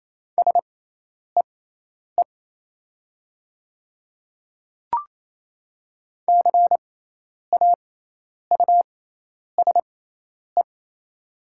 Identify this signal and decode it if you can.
Morse code, heard as a keyed tone, about 28 wpm, spaced as for 10 wpm, 700 Hz. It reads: SEE CAUSE
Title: 6x Character Spacing - Top 200 Words - 28wpm